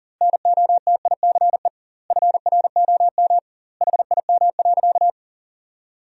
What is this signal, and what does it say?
Morse code, heard as a keyed tone, about 40 wpm, 700 Hz. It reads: NOTICE FROM HIM.
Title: 2x Word-Spacing - Sentences from Top 400 Words - 40wpm